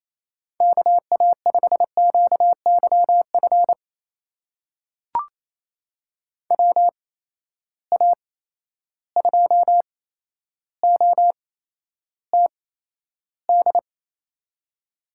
Morse code, heard as a keyed tone, about 28 wpm, spaced as for 12 wpm, 700 Hz. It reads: KA5QYF WA2OTD
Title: USA Call Signs - 6 Characters - 8x Character Spacing - 28wpm